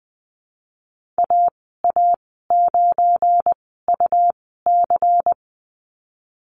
Morse code, heard as a keyed tone, about 20 wpm, spaced as for 15 wpm, 700 Hz. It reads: AA9UC
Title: 173 Lesson 28g - Number 9 - Call Signs 20wpm